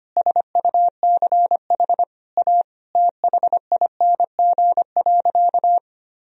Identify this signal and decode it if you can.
Morse code, heard as a keyed tone, about 25 wpm, 700 Hz. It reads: SUCH A THING.
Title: Sentences from Top 200 Words and Abbreviations 25wpm